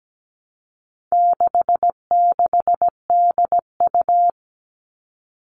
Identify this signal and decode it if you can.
Morse code, heard as a keyed tone, about 17 wpm, 700 Hz. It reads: 66DU
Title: All Cut Numbers - 4-Digit 17wpm